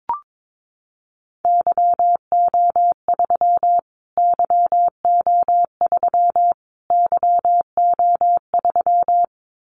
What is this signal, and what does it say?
Morse code, heard as a keyed tone, about 22 wpm, 700 Hz. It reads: YO3 YO3 YO3
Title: Callsign Prefixes for Top 25 DXCC - Repeated 3x - 22wpm